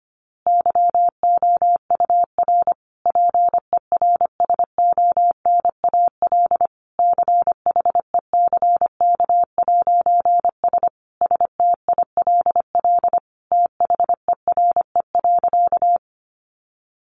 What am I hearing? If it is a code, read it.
Morse code, heard as a keyed tone, about 25 wpm, 700 Hz. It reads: YOUR PERSONAL CHECK'S STILL THERE.
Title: A Sound Of Thunder 25wpm